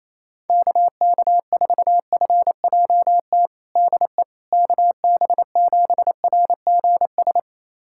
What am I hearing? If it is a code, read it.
Morse code, heard as a keyed tone, about 28 wpm, 700 Hz. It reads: KK4FJT DE KB7RGS
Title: USA Callsign DE Callsign - 6 Characters - 28wpm